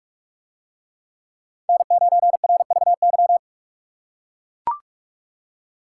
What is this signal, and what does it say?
Morse code, heard as a keyed tone, about 41 wpm, 700 Hz. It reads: N9RVY